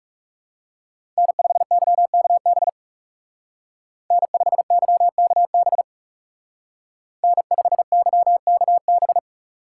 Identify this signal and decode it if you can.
Morse code, heard as a keyed tone, about 36 wpm, 700 Hz. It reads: N5YKB N5YKB N5YKB
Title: Speed Racing - USA Call Signs 30wpm